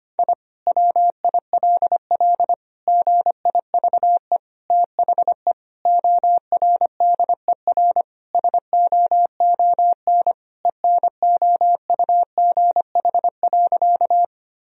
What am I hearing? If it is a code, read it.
Morse code, heard as a keyed tone, about 25 wpm, 700 Hz. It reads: I WILL GIVE THE ORDER SOON ENOUGH.
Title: Sentences from Top 1000 Words - Review Part 2 - 25wpm